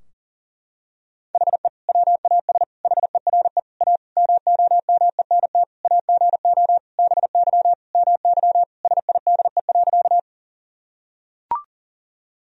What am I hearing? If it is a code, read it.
Morse code, heard as a keyed tone, about 40 wpm, 700 Hz. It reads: HE WAS HERE A MOMENT AGO BY MY SIDE.
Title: Sentences from Top 1000 Words - Review Part 3 - 40wpm